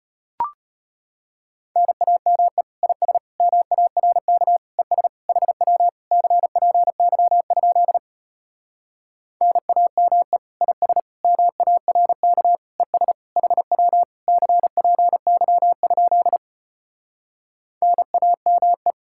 Morse code, heard as a keyed tone, about 33 wpm, 700 Hz. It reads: NAME IS MARK ES HW CPY? NAME IS MARK ES HW CPY? NAME